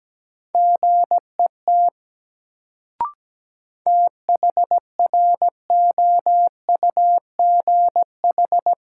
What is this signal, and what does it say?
Morse code, heard as a keyed tone, about 17 wpm, 700 Hz. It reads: GET THROUGH
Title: Sets of 2 Words - New Word Drill - 17wpm